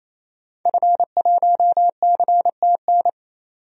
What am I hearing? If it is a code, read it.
Morse code, heard as a keyed tone, about 28 wpm, 700 Hz. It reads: F1CTN
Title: DXCC-100 Call Signs - 5 Characters - 28wpm